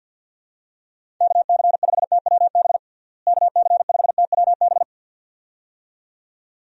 Morse code, heard as a keyed tone, about 51 wpm, 700 Hz. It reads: KC5TWB KC5TWB
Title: USA Call Signs - 6 Characters - Repeated 2x - 50wpm